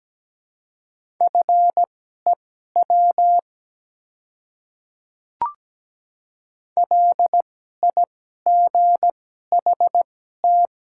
Morse code, heard as a keyed tone, about 17 wpm, spaced as for 12 wpm, 700 Hz. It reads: FEW LIGHT
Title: 2x Character Spacing - Top 300 Words - 17wpm